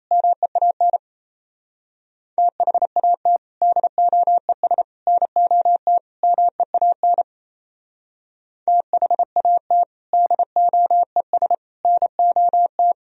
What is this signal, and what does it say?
Morse code, heard as a keyed tone, about 35 wpm, 700 Hz. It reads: MEAN THAT DOES NOT MEAN THAT DOES NOT
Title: Speed-Racing - Sets of 4 Words 28wpm